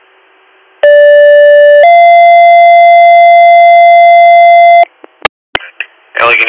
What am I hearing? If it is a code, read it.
Two-tone page, heard as a one-second tone, then a three-second tone, about 580 then 690 Hz, with speech. It is heard from 0.8 to 4.8 s.